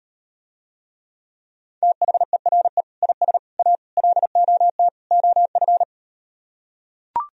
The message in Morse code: THERE IS A LOT OF

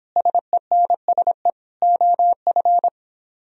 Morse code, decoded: SENSE OF

